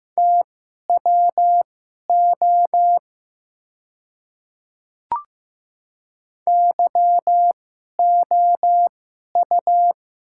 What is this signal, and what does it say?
Morse code, heard as a keyed tone, about 15 wpm, spaced as for 11 wpm, 700 Hz. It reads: TWO YOU
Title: 2x Character Spacing - Top 300 Words - 15wpm